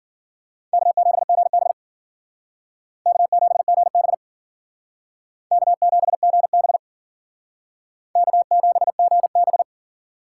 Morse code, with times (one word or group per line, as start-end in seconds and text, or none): 0.7-1.7 K7GB
3.1-4.1 K7GB
5.5-6.8 K7GB
8.2-9.6 K7GB